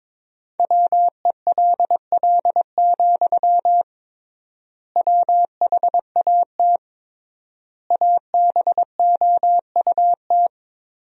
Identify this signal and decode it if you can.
Morse code, heard as a keyed tone, about 22 wpm, 700 Hz. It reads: WELL, WHAT ABOUT
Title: 3x Word-Spacing - Sentences from Top 500 Words - 22wpm